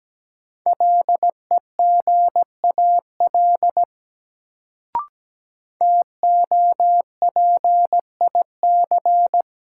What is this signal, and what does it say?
Morse code, heard as a keyed tone, about 17 wpm, 700 Hz. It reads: LEGAL TOPIC
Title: Sentences from Top 1200 Words - New Word Drill Alt - 17wpm